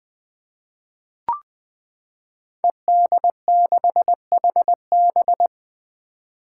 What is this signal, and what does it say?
Morse code, heard as a keyed tone, about 20 wpm, 700 Hz. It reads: ED6HB
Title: Five Challenging Letter-Number Combinations 20wpm